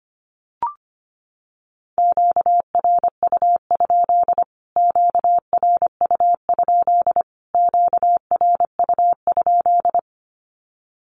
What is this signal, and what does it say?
Morse code, heard as a keyed tone, about 25 wpm, 700 Hz. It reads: QRU? QRU? QRU?
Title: Q-Codes Repeated 3x 25wpm